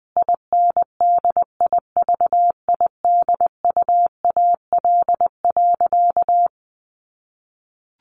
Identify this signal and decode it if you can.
Morse code, heard as a keyed tone, about 20 wpm, 700 Hz. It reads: INDIVIDUAL.
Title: Sentences from Top 1400 Words 20wpm